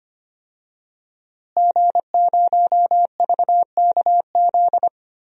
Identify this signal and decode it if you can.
Morse code, heard as a keyed tone, about 25 wpm, 700 Hz. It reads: G0VKZ
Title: DXCC-100 Call Signs - 5 Characters - Repeated 2x - 25wpm